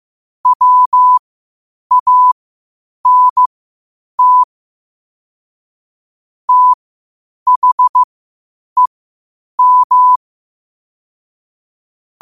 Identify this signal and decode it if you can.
Morse code, heard as a keyed tone, about 15 wpm, spaced as for 8 wpm, 1000 Hz. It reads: WANT THEM
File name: qotdfiles / 8wpm